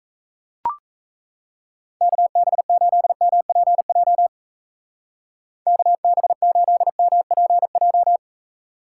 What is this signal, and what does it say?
Morse code, heard as a keyed tone, about 35 wpm, 700 Hz. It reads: KB8MPJ KB8MPJ